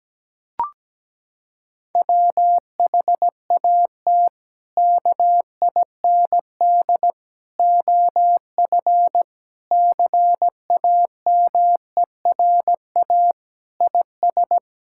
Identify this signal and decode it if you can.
Morse code, heard as a keyed tone, about 17 wpm, 700 Hz. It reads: WHAT KIND OF CAMERA IS
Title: Sentences from Top 1000 Words - Review Part 5 - 17wpm